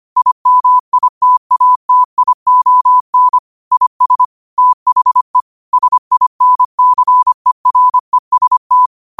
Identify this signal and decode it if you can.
Morse code, heard as a keyed tone, about 25 wpm, 1000 Hz. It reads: IMITATION IS THE SINCEREST
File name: qotdfiles / 25wpm